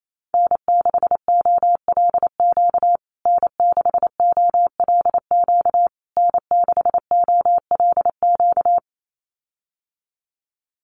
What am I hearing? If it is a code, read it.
Morse code, heard as a keyed tone, about 28 wpm, 700 Hz. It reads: N6OLQ N6OLQ N6OLQ
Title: USA Call Signs Repeated 3x 28wpm